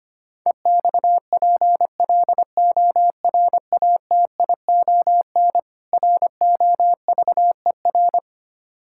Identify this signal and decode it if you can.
Morse code, heard as a keyed tone, about 26 wpm, 700 Hz. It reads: EXPLORATION ROVER
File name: practice / 25wpm / Difficult Words - Speed-Racing with Hints - 25wpm